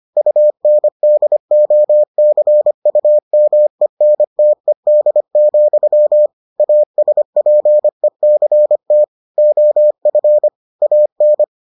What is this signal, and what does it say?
Morse code, heard as a keyed tone, about 25 wpm, 600 Hz. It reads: UNDOCUMENTED, ASPECT OF AN